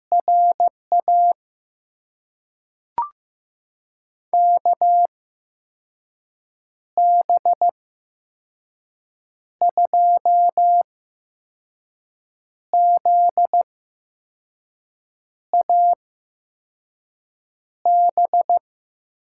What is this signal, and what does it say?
Morse code, heard as a keyed tone, about 15 wpm, spaced as for 5 wpm, 700 Hz. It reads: RA KB2ZAB
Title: USA Call Signs - 6 Characters - 8x Character Spacing - 15wpm